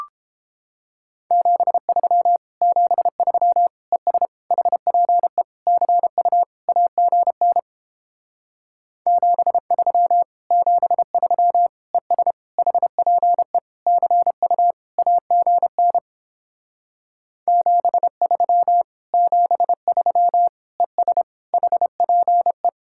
Morse code, heard as a keyed tone, about 31 wpm, 700 Hz. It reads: 73 73 ES HPE CU AGN 73 73 ES HPE CU AGN 73 73 ES HPE